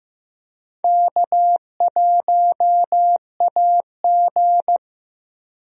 Morse code, heard as a keyed tone, about 15 wpm, 700 Hz. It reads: K1AG
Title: USA Call Signs - 4 Characters - Repeated 3x - 15wpm